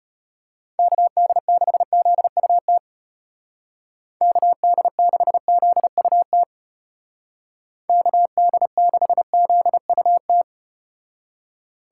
Morse code, heard as a keyed tone, about 32 wpm, 700 Hz. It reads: KD6ZUT KD6ZUT KD6ZUT